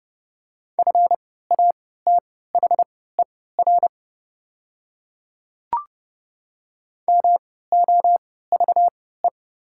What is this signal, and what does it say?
Morse code, heard as a keyed tone, about 29 wpm, spaced as for 17 wpm, 690 Hz. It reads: FATHER MOVE
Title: 3x Character Spacing - Top 200 Words - 30wpm